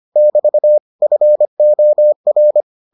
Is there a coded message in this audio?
Yes: Morse code from 0.2 to 2.6 s, heard as a keyed tone, about 25 wpm, 600 Hz. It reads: <BT> FOR